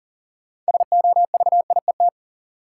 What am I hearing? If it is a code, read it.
Morse code, heard as a keyed tone, about 39 wpm, 700 Hz. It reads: SOVIET
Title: Sets of 5 Words Encore - New Word Drill - 40wpm